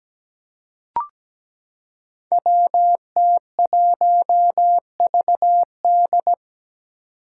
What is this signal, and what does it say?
Morse code, heard as a keyed tone, about 17 wpm, 700 Hz. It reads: WT1VD
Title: Five Challenging Letter-Number Combinations 17wpm